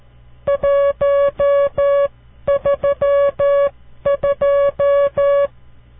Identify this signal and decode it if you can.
Morse code, heard as a keyed tone, about 13 wpm, spaced as for 11 wpm, 560 Hz. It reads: 132
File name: CW Learning Programs / farnsworth / unit 5